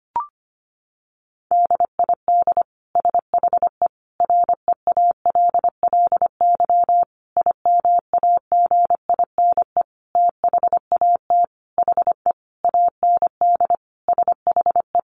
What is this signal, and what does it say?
Morse code, heard as a keyed tone, about 25 wpm, 700 Hz. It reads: DID SHE REALLY IMAGINE THAT HE AND SHE
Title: Sentences from Top 1400 Words 25wpm